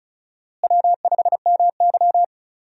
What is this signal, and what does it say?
Morse code, heard as a keyed tone, about 36 wpm, 700 Hz. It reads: W5MY
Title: USA Call Signs - 4 Characters - Repeated 3x - 35wpm